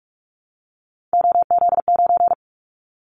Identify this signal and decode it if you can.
Morse code, heard as a keyed tone, about 46 wpm, 700 Hz. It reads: OZ9
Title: Callsign Prefixes for Top 100 DXCC - 45wpm